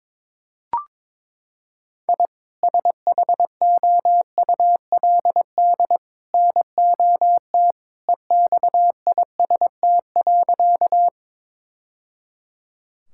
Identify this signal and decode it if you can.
Morse code, heard as a keyed tone, about 22 wpm, 700 Hz. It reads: I SHOULD NOT EXIST.
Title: Sentences from Top 1000 Words - Review Part 5 - 22wpm